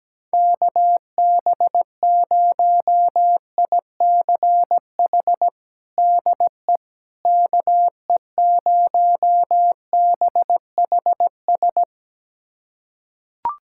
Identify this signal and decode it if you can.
Morse code, heard as a keyed tone, about 17 wpm, 700 Hz. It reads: KB0ICH DE KE0BHS
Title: USA Callsign DE Callsign - 6 Characters - 17wpm